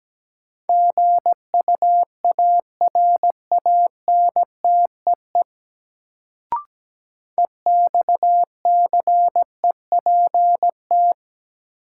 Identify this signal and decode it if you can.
Morse code, heard as a keyed tone, about 17 wpm, 700 Hz. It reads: GUARANTEE EXCEPT
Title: Sentences from Top 1600 Words - New Word Drill Alt - 17wpm